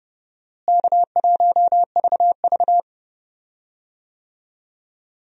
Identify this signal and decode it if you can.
Morse code, heard as a keyed tone, about 31 wpm, 700 Hz. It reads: K1VV
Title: USA Call Signs - 4 Characters - Repeated 3x - 30wpm